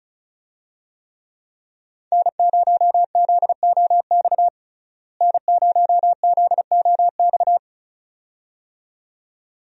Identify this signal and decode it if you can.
Morse code, heard as a keyed tone, about 36 wpm, 700 Hz. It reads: N0ZOX N0ZOX
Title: USA Call Signs - 5 Characters - Repeated 2x - 35wpm